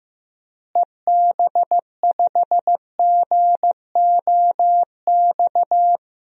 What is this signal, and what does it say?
Morse code, heard as a keyed tone, about 15 wpm, 700 Hz. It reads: EB5GOX
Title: DXCC-100 Call Signs - 6 Characters - Repeated 3x - 15wpm